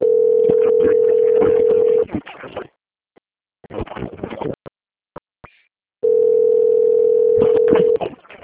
Ringback tone (North America), 440 + 480 Hz.
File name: out- - - - - .25346